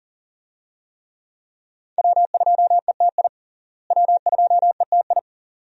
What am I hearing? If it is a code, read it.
Morse code, heard as a keyed tone, about 40 wpm, 700 Hz. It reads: W2ETI W2ETI
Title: USA Call Signs - 5 Characters - Repeated 2x - 40wpm